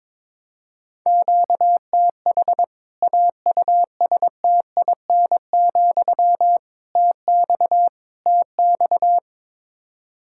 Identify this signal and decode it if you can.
Morse code, heard as a keyed tone, about 22 wpm, 700 Hz. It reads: QTH AUSTIN, TX TX
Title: K1USN SST QSO Practice 22wpm